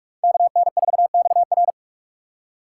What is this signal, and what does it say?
Morse code, heard as a keyed tone, about 44 wpm, 700 Hz. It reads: KN4XR